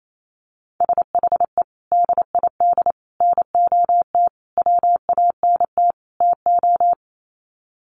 Morse code, heard as a keyed tone, about 28 wpm, 700 Hz. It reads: SHE DID NOT WANT TO
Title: Sets of 5 Words 28wpm